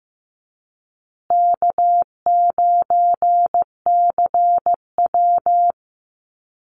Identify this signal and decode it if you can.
Morse code, heard as a keyed tone, about 15 wpm, 700 Hz. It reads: K9CW
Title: Easy Call Signs Repeated 3x - 15wpm